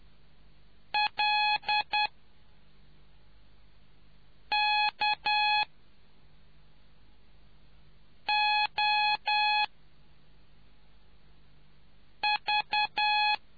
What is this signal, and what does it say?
Morse code, heard as a keyed tone, about 10 wpm, 840 Hz. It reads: L K O V